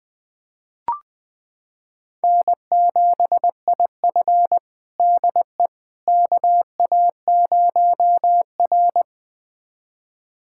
Morse code, heard as a keyed tone, about 20 wpm, 700 Hz. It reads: N7IF DE KA0R
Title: USA Callsign DE Callsign - 4 Characters - 20wpm